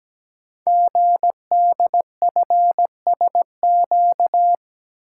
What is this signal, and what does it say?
Morse code, heard as a keyed tone, about 17 wpm, 700 Hz. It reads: GDFSQ
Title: Code Groups - 5 Letters - 17wpm